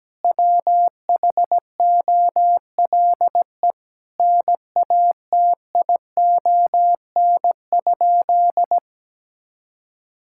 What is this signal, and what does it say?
Morse code, heard as a keyed tone, about 17 wpm, 700 Hz. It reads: WHOLE NATION?
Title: Sentences from Top 1000 Words - Review Part 6 - 17wpm